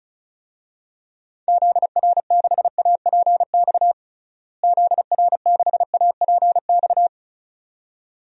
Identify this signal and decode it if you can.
Morse code, heard as a keyed tone, about 36 wpm, 700 Hz. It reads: ZR6APX ZR6APX
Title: DXCC-100 Call Signs - 6 Characters - Repeated 2x - 35wpm